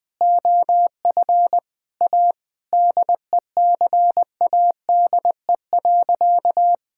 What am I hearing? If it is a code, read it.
Morse code, heard as a keyed tone, about 21 wpm, 700 Hz. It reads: OF A DECADE.
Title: Sentences with 7-Word Phrases - 20wpm